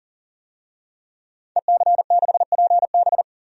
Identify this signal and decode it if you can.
Morse code, heard as a keyed tone, about 40 wpm, 700 Hz. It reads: EC6PB
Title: DXCC-100 Call Signs - 5 Characters - 40wpm